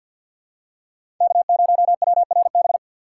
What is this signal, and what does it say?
Morse code, heard as a keyed tone, about 52 wpm, 700 Hz. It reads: K0WRB
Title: USA Call Signs - 2x Character Spacing - 50wpm